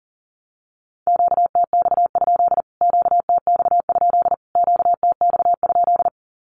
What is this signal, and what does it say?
Morse code, heard as a keyed tone, about 38 wpm, 700 Hz. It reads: QTX? QTX? QTX?
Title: Q-Codes Repeated 3x 40wpm